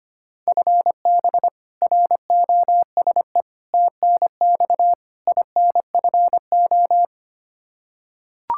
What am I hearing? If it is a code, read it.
Morse code, heard as a keyed tone, about 25 wpm, 700 Hz. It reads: FB ROSE TNX INFO
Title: Practice QSO - Exchange 3 - More Info - 25wpm